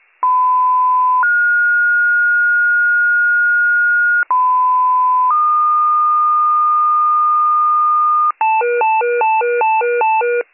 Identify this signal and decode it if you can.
Two-tone page, heard as a one-second tone, then a three-second tone, about 1010 then 1430 Hz, about 1010 then 1220 Hz, with noise. It is heard from 0.2 to 8.3 s.